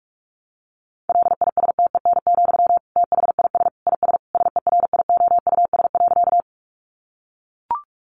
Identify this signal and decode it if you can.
Morse code, heard as a keyed tone, about 44 wpm, 700 Hz. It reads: LISTEN, THIS IS SERIOUS.
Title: Sentences from Top 1100 Words 45wpm